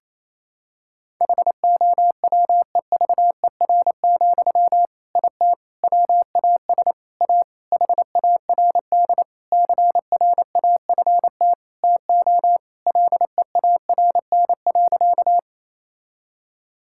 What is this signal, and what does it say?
Morse code, heard as a keyed tone, about 28 wpm, 700 Hz. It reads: HOWEVER, IT WAS A HARD CRAFT TO LEARN.
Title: Sentences from Top 1000 Words - Review Part 6 - 28wpm